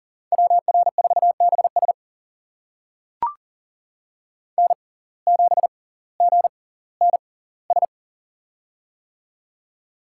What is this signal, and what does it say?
Morse code, heard as a keyed tone, about 40 wpm, spaced as for 20 wpm, 700 Hz. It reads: WR4BS N7GNS
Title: USA Call Signs - 5 Characters - 6x Character Spacing - 40wpm